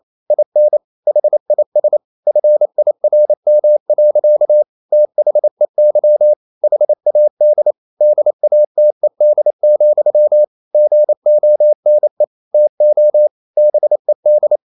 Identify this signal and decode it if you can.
Morse code, heard as a keyed tone, about 28 wpm, 600 Hz. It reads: IN HIS FIRM. THEY HAD DATED, GONE TO BED
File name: Never 0027wpm